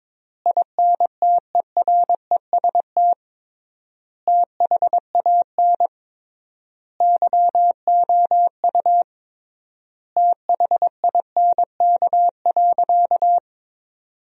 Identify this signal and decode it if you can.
Morse code, heard as a keyed tone, about 22 wpm, 700 Hz. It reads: INTEREST THAN YOU THINK.
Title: 3x Word-Spacing - Sentences from Top 400 Words - 22wpm